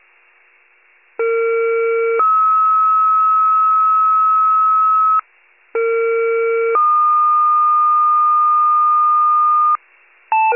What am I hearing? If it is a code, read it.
Two-tone page, heard as a one-second tone, then a three-second tone, about 460 then 1290 Hz, about 460 then 1220 Hz. It is heard from 1.2 to 9.8 s.
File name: pager / 21 RESCUE / wcalert